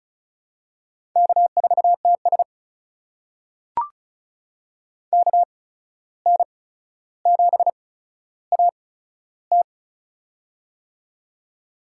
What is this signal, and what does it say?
Morse code, heard as a keyed tone, about 36 wpm, spaced as for 14 wpm, 700 Hz. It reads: K4TS KN7AT
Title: USA Call Signs - 8x Character Spacing - 35wpm